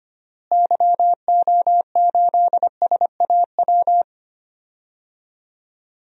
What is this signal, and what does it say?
Morse code, heard as a keyed tone, about 25 wpm, 700 Hz. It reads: YO8SAW
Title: DXCC-100 Call Signs - 6 Characters - Repeated 3x - 25wpm